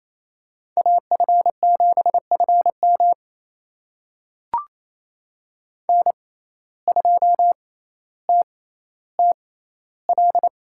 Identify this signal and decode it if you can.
Morse code, heard as a keyed tone, about 28 wpm, spaced as for 14 wpm, 700 Hz. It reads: AF7FM N2TTL
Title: USA Call Signs - 5 Characters - 6x Character Spacing - 28wpm